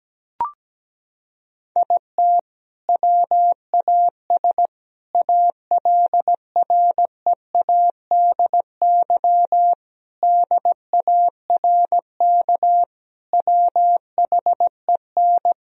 Morse code, heard as a keyed tone, about 17 wpm, 700 Hz. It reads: IT WAS ALREADY DARK WHEN
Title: Sentences from Top 1000 Words - Review Part 2 - 17wpm